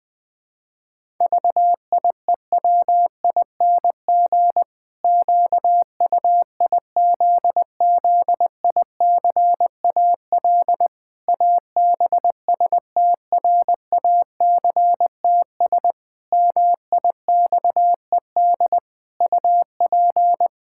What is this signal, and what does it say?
Morse code, heard as a keyed tone, about 20 wpm, 700 Hz. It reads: VIEWING QUIZZICAL ABSTRACTS MIXED UP
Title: Pangram Challenge v2 - 20wpm